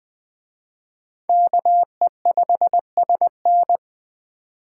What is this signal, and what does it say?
Morse code, heard as a keyed tone, about 24 wpm, 700 Hz. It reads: KE5SN
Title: Speed Racing - USA Call Signs 20wpm